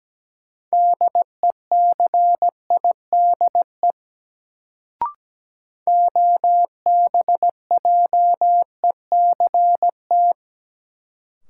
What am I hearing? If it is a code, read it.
Morse code, heard as a keyed tone, about 17 wpm, 700 Hz. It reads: DECIDE OBJECT
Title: Sentences from Top 500 Words - New Word Drill - 17wpm